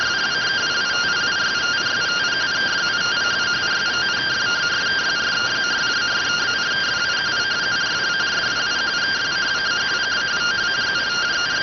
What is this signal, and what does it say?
Two-tone teleprinter signal, about 170 Hz shift.